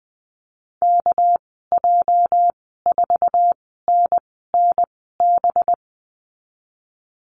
Morse code, heard as a keyed tone, about 20 wpm, spaced as for 15 wpm, 700 Hz. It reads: KJ4NNB
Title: 207 Lesson 33g - Letter j - Call Signs 20wpm